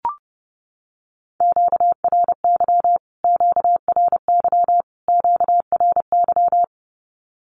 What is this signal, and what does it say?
Morse code, heard as a keyed tone, about 30 wpm, 700 Hz. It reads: QRY QRY QRY
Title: Q-Codes Repeated 3x 30wpm